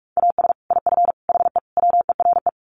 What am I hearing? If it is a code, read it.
Morse code, heard as a keyed tone, about 44 wpm, 700 Hz. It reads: AS IF HE WERE
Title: Sets of 4 Words 45wpm